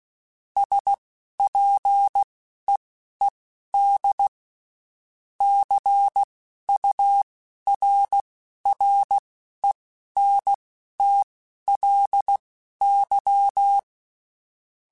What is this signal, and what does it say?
Morse code, heard as a keyed tone, about 16 wpm, spaced as for 12 wpm, 800 Hz. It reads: SPEED CURRENTLY
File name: Spaced Characters / Spaced Characters-mp3s / Weather 5 12 sp @15